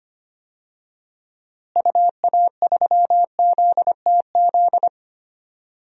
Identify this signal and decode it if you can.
Morse code, heard as a keyed tone, about 25 wpm, 700 Hz. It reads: UA3ZTZ
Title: DXCC-100 Call Signs - 6 Characters - Repeated 2x - 25wpm